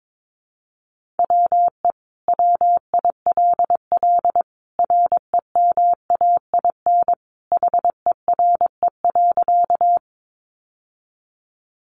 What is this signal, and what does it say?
Morse code, heard as a keyed tone, about 22 wpm, 700 Hz. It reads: WE WILL REMAIN HERE.
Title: Sentences from Top 1200 Words 22wpm